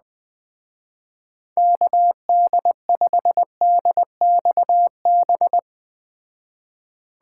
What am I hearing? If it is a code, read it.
Morse code, heard as a keyed tone, about 20 wpm, 700 Hz. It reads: KD5DXB
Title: Speed Racing - USA 2x3 Call Signs 17wpm